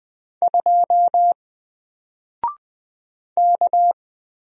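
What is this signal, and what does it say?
Morse code, heard as a keyed tone, about 20 wpm, 700 Hz. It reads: 2 K
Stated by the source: Single letter and number Morse drill at 20 wpm, from N4LCD.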